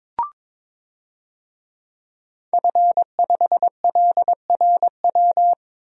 Morse code, heard as a keyed tone, about 22 wpm, 700 Hz. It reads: F5LRW
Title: DXCC-100 Call Signs - 5 Characters - Repeated 2x - 22wpm